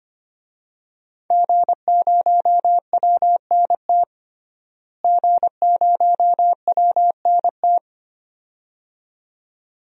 Morse code, heard as a keyed tone, about 25 wpm, 700 Hz. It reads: G0WNT G0WNT
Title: DXCC-100 Call Signs - 5 Characters - Repeated 2x - 25wpm